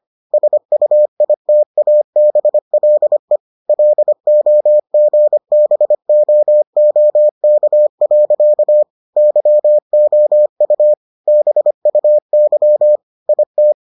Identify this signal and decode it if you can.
Morse code, heard as a keyed tone, about 25 wpm, 600 Hz. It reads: SUITABLE LOGBOOK. YOU BUY IT